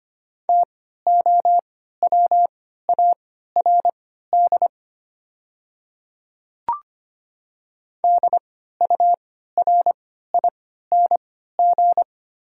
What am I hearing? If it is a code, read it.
Morse code, heard as a keyed tone, about 25 wpm, spaced as for 14 wpm, 690 Hz. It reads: TOWARD DURING
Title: 3x Character Spacing - Top 500 Words - 25wpm